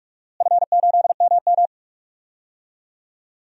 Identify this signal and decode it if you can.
Morse code, heard as a keyed tone, about 46 wpm, 700 Hz. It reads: F8MM
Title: DXCC-100 Call Signs - 4 Characters - Repeated 2x - 45wpm